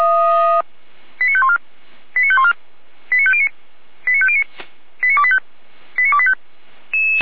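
Five-tone selective call (ZVEI), heard as quick runs of five tones, tones of about 70 ms. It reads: 89514 89514 89509 89509 89285 89285
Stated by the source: ZVEI five-tone selective call 89220 received by SDR, from the Feuerwehr Orlamünde audio server.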